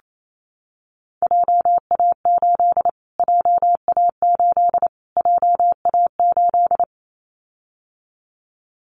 Morse code, heard as a keyed tone, about 28 wpm, 700 Hz. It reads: JA8 JA8 JA8
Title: Callsign Prefixes for Top 25 DXCC - Repeated 3x - 28wpm